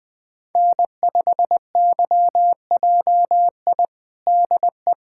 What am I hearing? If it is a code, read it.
Morse code, heard as a keyed tone, about 20 wpm, 700 Hz. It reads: N5YJI DE
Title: USA Callsign DE Callsign - 5 Characters - 20wpm